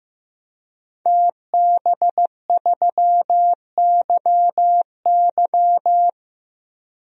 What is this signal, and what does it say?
Morse code, heard as a keyed tone, about 16 wpm, 700 Hz. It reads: TB3YY
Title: DXCC-100 Call Signs - 5 Characters - Repeated 2x - 15wpm